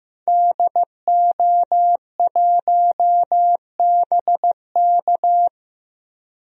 Morse code, heard as a keyed tone, about 15 wpm, 700 Hz. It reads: DO1BK
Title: DXCC-100 Call Signs - 5 Characters - Repeated 3x - 15wpm